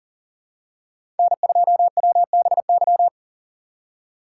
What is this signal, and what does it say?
Morse code, heard as a keyed tone, about 40 wpm, 700 Hz. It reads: N2WBY
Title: USA Call Signs - 4x Character Spacing - 40wpm